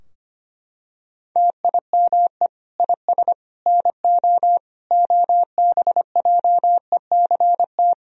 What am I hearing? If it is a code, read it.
Morse code, heard as a keyed tone, about 25 wpm, 700 Hz. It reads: TIME IS NO OBJECT
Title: Sentences from Top 1000 Words - Review Part 3 - 25wpm